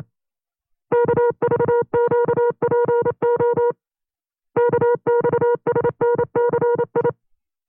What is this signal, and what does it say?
Morse code, heard as a keyed tone, about 29 wpm, 480 Hz. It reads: KVQPO KXSNCI
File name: FreeFiles / CW-practice / 25WPM-side2